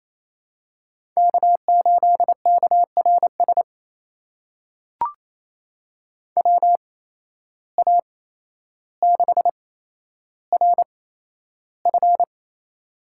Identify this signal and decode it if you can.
Morse code, heard as a keyed tone, about 28 wpm, spaced as for 12 wpm, 700 Hz. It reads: K8KRS WA6RF
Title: USA Call Signs - 5 Characters - 8x Character Spacing - 28wpm